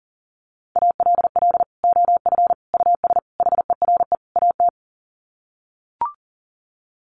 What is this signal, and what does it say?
Morse code, heard as a keyed tone, about 40 wpm, 700 Hz. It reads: ALL OF US HERE AT
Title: Sets of 5 Words 40wpm